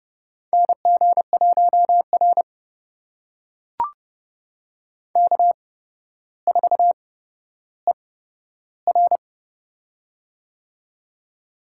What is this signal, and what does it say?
Morse code, heard as a keyed tone, about 30 wpm, spaced as for 13 wpm, 700 Hz. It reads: NG1R K4ER
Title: USA Call Signs - 8x Character Spacing - 30wpm